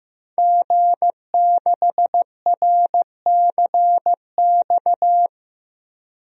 Morse code, heard as a keyed tone, about 15 wpm, 700 Hz. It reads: G6RCX
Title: DXCC-100 Call Signs - 5 Characters - Repeated 3x - 15wpm